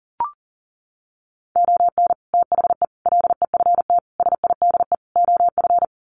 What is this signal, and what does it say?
Morse code, heard as a keyed tone, about 40 wpm, 700 Hz. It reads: ON THE LEFT SIDE OF
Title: Sets of 5 Words Encore2 40wpm